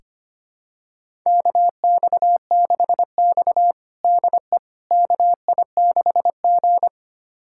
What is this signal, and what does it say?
Morse code, heard as a keyed tone, about 25 wpm, 700 Hz. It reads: KX6X DE KI6G
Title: USA Callsign DE Callsign - 4 Characters - 25wpm